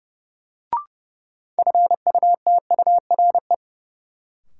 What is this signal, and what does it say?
Morse code, heard as a keyed tone, about 31 wpm, 700 Hz. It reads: FUTURE
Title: Sentences from Top 700 Words - New Word Drill - 30wpm